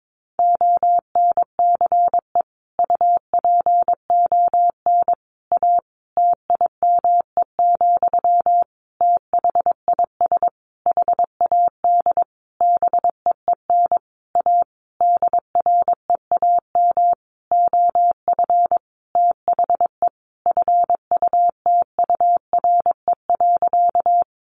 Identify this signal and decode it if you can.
Morse code, heard as a keyed tone, about 22 wpm, 700 Hz. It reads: ONCE UPON A TIME, THIS HAD BEEN A DREAM OF THE FUTURE.
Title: Sentences from Top 1500 Words 22wpm